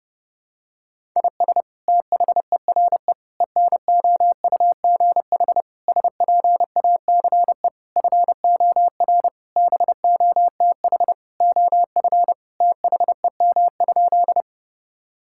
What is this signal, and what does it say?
Morse code, heard as a keyed tone, about 32 wpm, 700 Hz. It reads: IS THERE ENOUGH SPACE FOR BOTH OF THEM?
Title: Sentences from Top 1000 Words - Review Part 5 - 30wpm